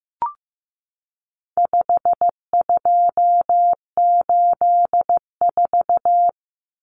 Morse code, heard as a keyed tone, about 15 wpm, 700 Hz. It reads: 5284
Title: Common Cut Numbers - 4-Digit 15wpm